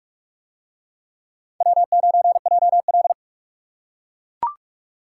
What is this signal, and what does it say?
Morse code, heard as a keyed tone, about 39 wpm, 700 Hz. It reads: W9JL